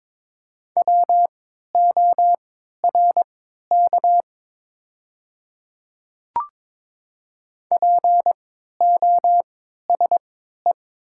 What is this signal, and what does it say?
Morse code, heard as a keyed tone, about 22 wpm, spaced as for 12 wpm, 690 Hz. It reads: WORK POSE
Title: 3x Character Spacing - Top 500 Words - 22wpm